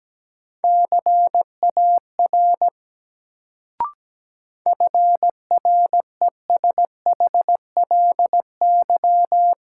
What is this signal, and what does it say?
Morse code, heard as a keyed tone, about 17 wpm, 700 Hz. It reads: CAR FRESHLY
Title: Sets of 5 Words Encore - New Word Drill - 17wpm